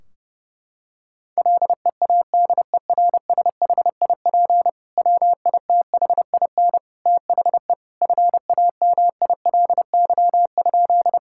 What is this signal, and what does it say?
Morse code, heard as a keyed tone, about 30 wpm, 700 Hz. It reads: LEADERSHIP WITHIN THE FAMILY?
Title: Sentences from Top 1600 Words 30wpm